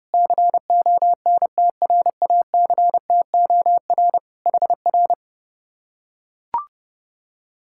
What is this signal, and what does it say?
Morse code, heard as a keyed tone, about 29 wpm, 700 Hz. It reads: CONTRACTOR HR